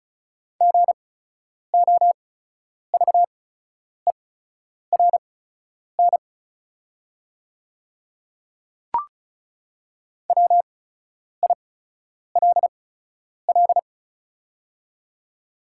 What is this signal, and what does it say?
Morse code, heard as a keyed tone, about 34 wpm, spaced as for 10 wpm, 700 Hz. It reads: GOVERN WILL
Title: 8x Character Spacing - Top 500 Words - 35wpm